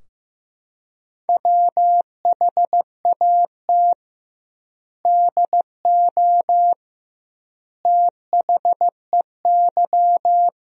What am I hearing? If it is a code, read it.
Morse code, heard as a keyed tone, about 15 wpm, 700 Hz. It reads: WHAT DO THEY
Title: 2x Word-Spacing - Sentences from Top 200 Words - 15wpm